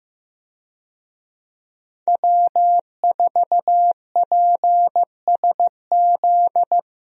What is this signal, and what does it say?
Morse code, heard as a keyed tone, about 15 wpm, 700 Hz. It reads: W4PSZ
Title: USA Call Signs - 5 Characters - Repeated 3x - 15wpm